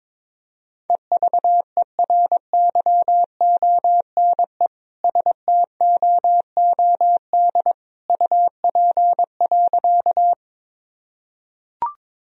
Morse code, heard as a keyed tone, about 22 wpm, 700 Hz. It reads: EVERYONE STOOD UP.
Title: Sentences from Top 1000 Words - Review Part 2 - 22wpm